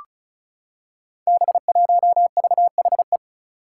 Morse code, heard as a keyed tone, about 36 wpm, 700 Hz. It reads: B1VHE
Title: Five Challenging Letter-Number Combinations 35wpm